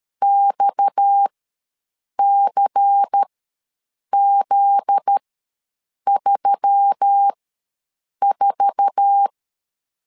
Morse code, heard as a keyed tone, about 12 wpm, spaced as for 7 wpm, 800 Hz. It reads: XCZ34
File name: LEZIONI / 9 2867 ws 12wpm 3431